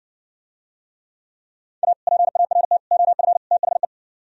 Morse code, heard as a keyed tone, about 69 wpm, 700 Hz. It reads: A PART OF THE